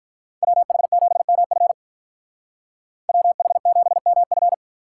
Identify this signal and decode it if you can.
Morse code, heard as a keyed tone, about 43 wpm, 700 Hz. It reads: WH7MF WH7MF
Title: Speed Racing - USA 2x2 Call Signs 35wpm